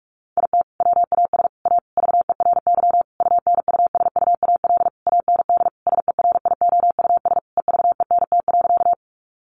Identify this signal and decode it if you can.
Morse code, heard as a keyed tone, about 44 wpm, 700 Hz. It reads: IT WAS A VERY UNUSUAL AND SERIOUS EVENT.
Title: Sentences from Top 1400 Words 45wpm